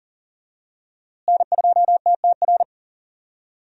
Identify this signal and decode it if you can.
Morse code, heard as a keyed tone, about 40 wpm, 700 Hz. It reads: N2TTR
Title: USA Call Signs - 5 Characters - Repeated 2x - 40wpm